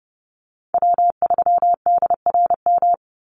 Morse code, heard as a keyed tone, about 30 wpm, 700 Hz. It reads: W3DRM
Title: USA Call Signs 30wpm